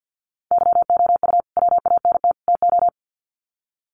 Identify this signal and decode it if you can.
Morse code, heard as a keyed tone, about 50 wpm, 700 Hz. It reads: YOU WANT TO
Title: Sets of 3 Words 50wpm